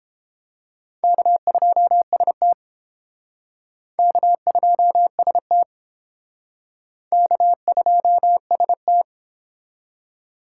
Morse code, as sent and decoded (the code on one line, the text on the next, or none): -.- ..--- ... - / -.- ..--- ... - / -.- ..--- ... -
K2ST K2ST K2ST